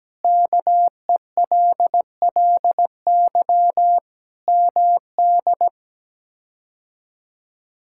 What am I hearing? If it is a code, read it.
Morse code, heard as a keyed tone, about 17 wpm, 700 Hz. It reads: KELLY MD
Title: K1USN SST QSO Practice - 17wpm